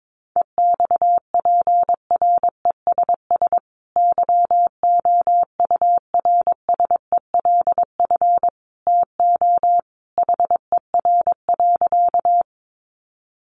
Morse code, heard as a keyed tone, about 22 wpm, 700 Hz. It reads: EXPRESS YOURSELF TO HER.
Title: Sentences from Top 1100 Words 22wpm